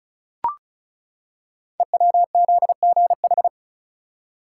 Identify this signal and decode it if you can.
Morse code, heard as a keyed tone, about 36 wpm, 690 Hz. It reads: EWZGH
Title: Code Groups - 5 Letters - 35wpm